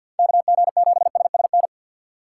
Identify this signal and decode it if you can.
Morse code, heard as a keyed tone, about 46 wpm, 700 Hz. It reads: KG7SSN
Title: USA Call Signs - 6x Character Spacing - 50wpm